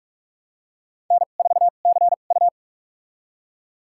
Morse code, heard as a keyed tone, about 46 wpm, spaced as for 33 wpm, 700 Hz. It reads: N4CU
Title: USA Call Signs - 4 Characters - 2x Character Spacing - 45wpm